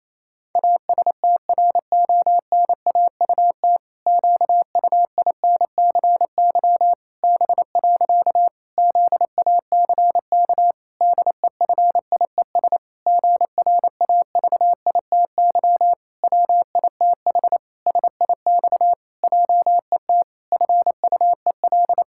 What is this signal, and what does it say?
Morse code, heard as a keyed tone, about 28 wpm, 700 Hz. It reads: ASTRONAUT QUINCY B. ZACK DEFIES GRAVITY WITH SIX JET FUEL
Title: Pangram Challenge v2 - 28wpm